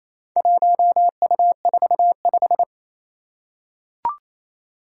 Morse code, heard as a keyed tone, about 28 wpm, 700 Hz. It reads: 1U45
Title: Four Challenging Letter-Number Combinations 28wpm